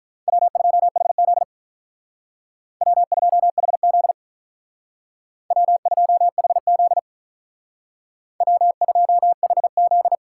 Morse code, heard as a keyed tone, about 43 wpm, 700 Hz. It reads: W2HZ W2HZ W2HZ W2HZ
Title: Speed Racing - USA 1x2 Call Signs 35wpm